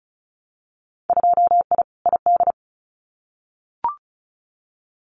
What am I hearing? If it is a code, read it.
Morse code, heard as a keyed tone, about 34 wpm, 700 Hz. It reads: 2I ID
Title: Winter Field Day Exchanges Encore 35wpm